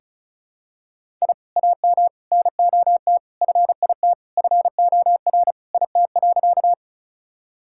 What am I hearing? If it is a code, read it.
Morse code, heard as a keyed tone, about 36 wpm, 700 Hz. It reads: I AM NOT FIT FOR IT.
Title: Sentences from Top 1000 Words - Review Part 5 - 35wpm